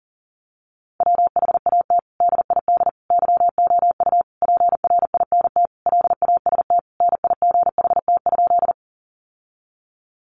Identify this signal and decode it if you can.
Morse code, heard as a keyed tone, about 40 wpm, 700 Hz. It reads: WHAT DID YOU PRINT LAST NIGHT?
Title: Sentences from Top 1400 Words 40wpm